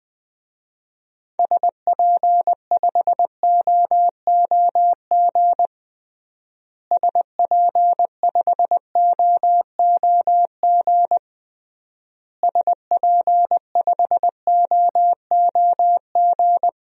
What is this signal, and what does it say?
Morse code, heard as a keyed tone, about 21 wpm, 700 Hz. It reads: SP5OOG SP5OOG SP5OOG
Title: DXCC-100 Call Signs - 6 Characters - Repeated 3x - 20wpm